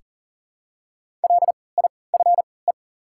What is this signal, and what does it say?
Morse code, heard as a keyed tone, about 40 wpm, spaced as for 23 wpm, 700 Hz. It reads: LIFE
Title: 3x Character Spacing - Top 300 Words - 40wpm